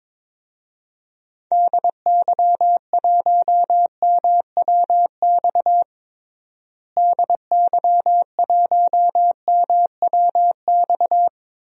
Morse code, heard as a keyed tone, about 22 wpm, 700 Hz. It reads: DY1MWX DY1MWX
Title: DXCC-100 Call Signs - 6 Characters - Repeated 3x - 22wpm